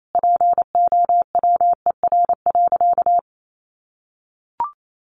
Morse code, heard as a keyed tone, about 28 wpm, 700 Hz. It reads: POWER.
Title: Sentences from Top 1200 Words 28wpm